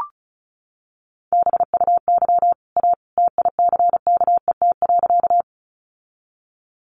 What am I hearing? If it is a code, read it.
Morse code, heard as a keyed tone, about 34 wpm, 700 Hz. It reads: BUY A TICKET.
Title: Sentences from Top 1400 Words 35wpm